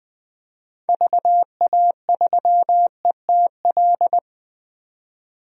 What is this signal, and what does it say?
Morse code, heard as a keyed tone, about 20 wpm, 700 Hz. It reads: VA3ETL
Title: DXCC-100 Call Signs - 6 Characters - 20wpm